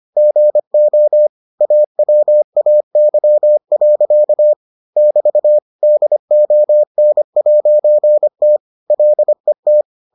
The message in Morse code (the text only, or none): GO AWAY. <BT> DON'T LET